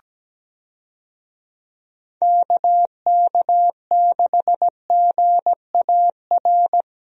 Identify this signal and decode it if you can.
Morse code, heard as a keyed tone, about 17 wpm, 700 Hz. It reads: KK6GAR
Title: USA Call Signs - 6 Characters - Repeated 2x - 17wpm